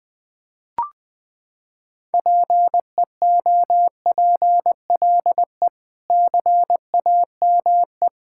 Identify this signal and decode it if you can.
Morse code, heard as a keyed tone, about 21 wpm, 700 Hz. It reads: PEOPLE CAME
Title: Sentences with 5-Word Phrases - 20wpm